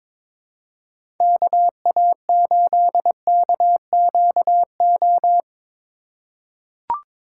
Morse code, heard as a keyed tone, about 22 wpm, 700 Hz. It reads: KA8KQO
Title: USA Call Signs - 3x Character Spacing - 22wpm